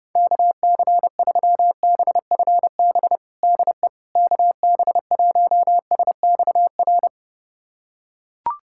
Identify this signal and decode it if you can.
Morse code, heard as a keyed tone, about 30 wpm, 700 Hz. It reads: KC3BFB DE KB1SXR
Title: USA Callsign DE Callsign - 6 Characters - 30wpm